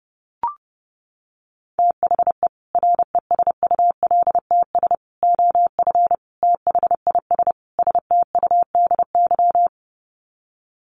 Morse code, heard as a keyed tone, about 30 wpm, 700 Hz. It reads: THE RESULTS OF THIS STUDY
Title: Sets of 5 Words 30wpm